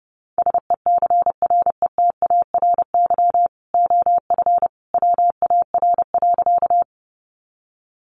Morse code, heard as a keyed tone, about 30 wpm, 700 Hz. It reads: SECRETARY OF WAR.
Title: Sentences from Top 1300 Words 30wpm